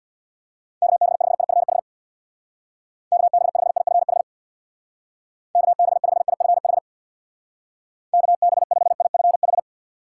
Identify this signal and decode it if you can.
Morse code, heard as a keyed tone, about 63 wpm, 700 Hz. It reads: KB5IFH KB5IFH KB5IFH KB5IFH